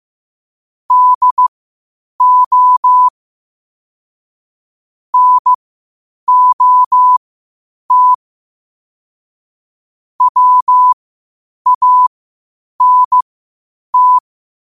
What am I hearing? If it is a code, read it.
Morse code, heard as a keyed tone, about 15 wpm, spaced as for 8 wpm, 1000 Hz. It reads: DO NOT WANT
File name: qotdfiles / 8wpm